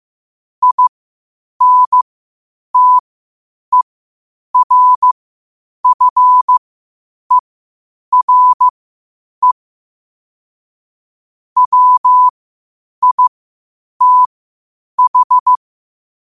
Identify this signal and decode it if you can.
Morse code, heard as a keyed tone, about 15 wpm, spaced as for 8 wpm, 1000 Hz. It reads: INTERFERE WITH
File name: qotdfiles / 8wpm